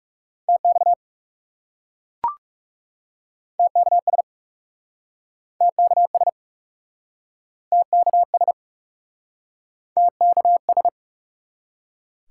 Morse code, heard as a keyed tone, about 40 wpm, 700 Hz. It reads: TX TKS TKS TKS TKS